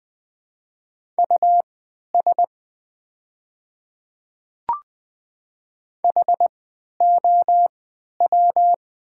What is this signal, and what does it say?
Morse code, heard as a keyed tone, about 20 wpm, spaced as for 11 wpm, 700 Hz. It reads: US HOW
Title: 3x Character Spacing - Top 200 Words - 20wpm